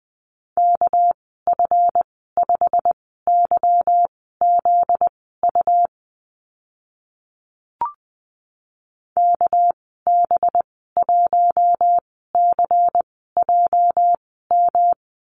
Morse code, heard as a keyed tone, about 20 wpm, spaced as for 15 wpm, 700 Hz. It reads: KF5YZU KB1CJM
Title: 245 Lesson 39g - Letter z - Call Signs 20wpm